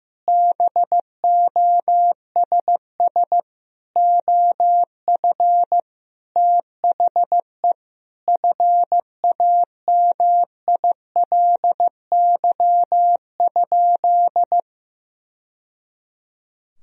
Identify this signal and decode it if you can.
Morse code, heard as a keyed tone, about 16 wpm, 700 Hz. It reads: BOSS OF THE FAMILY?
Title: Sentences from Top 1000 Words - Review Part 4 - 15wpm